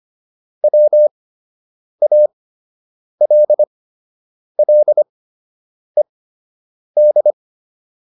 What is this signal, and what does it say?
Morse code, heard as a keyed tone, about 25 wpm, spaced as for 8 wpm, 600 Hz. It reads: WALLED
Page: fundamental-practice-files